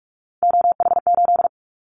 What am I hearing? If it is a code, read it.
Morse code, heard as a keyed tone, about 44 wpm, 700 Hz. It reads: OH8
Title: Callsign Prefixes for Top 25 DXCC - 45wpm